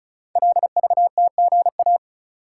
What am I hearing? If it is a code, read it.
Morse code, heard as a keyed tone, about 36 wpm, 700 Hz. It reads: LVTGA
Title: Code Groups - 5 Letters - 35wpm